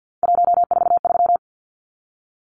Morse code, heard as a keyed tone, about 50 wpm, 700 Hz. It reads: 143
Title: Numbers - 3-Digit 50wpm